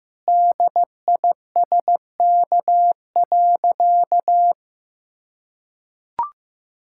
Morse code, heard as a keyed tone, about 15 wpm, 700 Hz. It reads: DISK.
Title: Sentences from Top 1000 Words - Review Part 6 - 15wpm